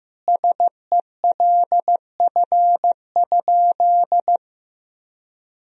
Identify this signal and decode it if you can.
Morse code, heard as a keyed tone, about 15 wpm, 700 Hz. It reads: SELF?
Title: Sentences from Top 1000 Words - Review Part 1 - 15wpm